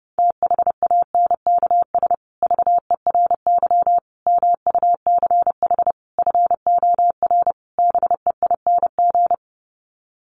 Morse code, heard as a keyed tone, about 30 wpm, 700 Hz. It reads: THANKS VERY MUCH FOR BEING
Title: Sets of 5 Words Encore 30wpm